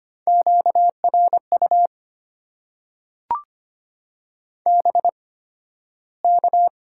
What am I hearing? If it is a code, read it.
Morse code, heard as a keyed tone, about 25 wpm, spaced as for 13 wpm, 690 Hz. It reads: QRU BK
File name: Top 90 QSO Elements - 8x Character Spacing - 25wpm